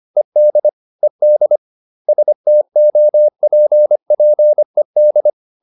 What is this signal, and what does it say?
Morse code, heard as a keyed tone, about 25 wpm, 600 Hz. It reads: ED ED STOPPED